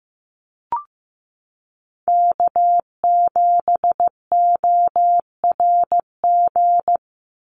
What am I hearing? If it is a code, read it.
Morse code, heard as a keyed tone, about 15 wpm, 700 Hz. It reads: K7ORG
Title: NA SOTA Chaser Callsigns 15wpm